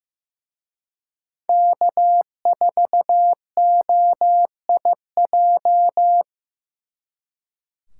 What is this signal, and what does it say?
Morse code, heard as a keyed tone, about 18 wpm, 700 Hz. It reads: K4OIJ
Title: Speed Racing - USA Call Signs 15wpm